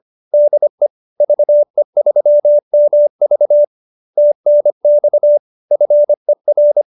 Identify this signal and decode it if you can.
Morse code, heard as a keyed tone, about 25 wpm, 590 Hz. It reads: DE VE3MV TNX FER